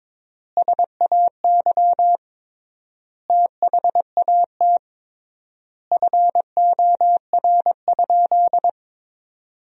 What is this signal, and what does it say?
Morse code, heard as a keyed tone, about 22 wpm, 700 Hz. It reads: SAY THAT FOR?
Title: 3x Word-Spacing - Sentences from Top 200 Words - 22wpm